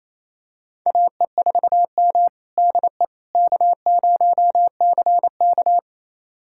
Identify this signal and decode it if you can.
Morse code, heard as a keyed tone, about 28 wpm, 700 Hz. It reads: AE4M DE K0CK
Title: USA Callsign DE Callsign - 4 Characters - 28wpm